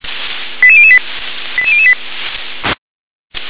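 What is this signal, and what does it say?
Five-tone selective call (ZVEI), heard as quick runs of five tones, tones of about 70 ms. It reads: 88008 88008
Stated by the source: SDR-received ZVEI five-tone selective call, address 88783.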